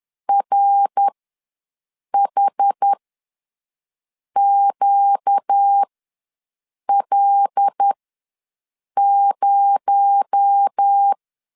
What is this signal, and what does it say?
Morse code, heard as a keyed tone, about 10 wpm, spaced as for 6 wpm, 800 Hz. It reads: RHQL0
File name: LEZIONI / 7 GF9Z / 7 GF9Z ws 10wpm 3919